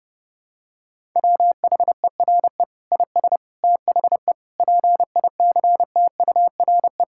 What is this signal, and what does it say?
Morse code, heard as a keyed tone, about 32 wpm, 700 Hz. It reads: WHERE IS THE PICTURE
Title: Sentences from Top 1000 Words - Review Part 6 - 30wpm